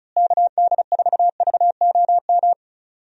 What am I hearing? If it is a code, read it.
Morse code, heard as a keyed tone, about 34 wpm, 700 Hz. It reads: KD4VOM
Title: USA Call Signs - 6 Characters - Repeated 3x - 35wpm